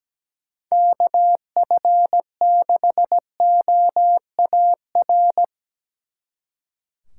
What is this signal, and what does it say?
Morse code, heard as a keyed tone, about 19 wpm, 700 Hz. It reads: KF6OAR
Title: Speed Racing - USA 2x3 Call Signs 17wpm